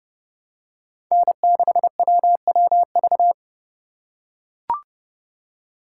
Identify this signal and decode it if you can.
Morse code, heard as a keyed tone, about 31 wpm, 700 Hz. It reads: N6WWV
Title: USA Call Signs - 5 Characters - 3x Character Spacing - 30wpm